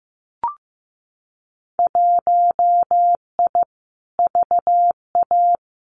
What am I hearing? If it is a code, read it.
Morse code, heard as a keyed tone, about 15 wpm, 700 Hz. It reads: 1I VA
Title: Winter Field Day Exchanges 15wpm